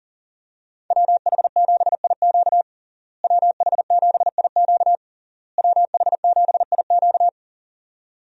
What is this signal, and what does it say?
Morse code, heard as a keyed tone, about 40 wpm, 700 Hz. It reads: WH7IQ WH7IQ WH7IQ
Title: USA Call Signs - 5 Characters - Repeated 3x - 40wpm